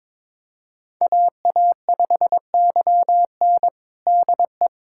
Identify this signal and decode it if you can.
Morse code, heard as a keyed tone, about 22 wpm, 700 Hz. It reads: AA5YN DE
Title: USA Callsign DE Callsign - 5 Characters - 22wpm